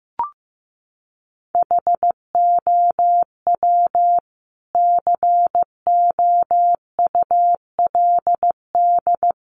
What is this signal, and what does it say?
Morse code, heard as a keyed tone, about 15 wpm, 700 Hz. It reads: HOW COULD
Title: Sentences from Top 1200 Words 15wpm